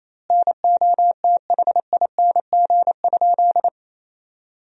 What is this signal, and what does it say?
Morse code, heard as a keyed tone, about 28 wpm, 700 Hz. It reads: NOTHING?
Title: 2x Word-Spacing - Sentences from Top 500 Words - 28wpm